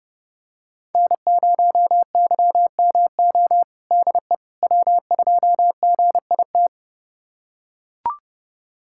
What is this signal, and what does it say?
Morse code, heard as a keyed tone, about 30 wpm, 700 Hz. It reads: N0YMO DE W2GIT
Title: USA Callsign DE Callsign - 5 Characters - 30wpm